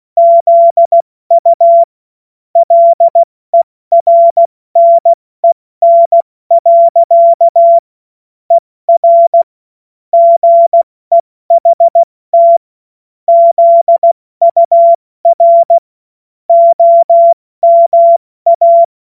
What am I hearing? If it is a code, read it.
Morse code, heard as a keyed tone, about 16 wpm, 680 Hz. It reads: ZU LERNEN. ER GEHT ZUR OMA
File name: Morsekurs / Witze / Witz 24. August 2021, 70 F80